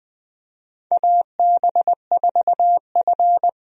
Morse code, heard as a keyed tone, about 20 wpm, 700 Hz. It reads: AB4F
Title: USA Call Signs - 4 Characters - 20wpm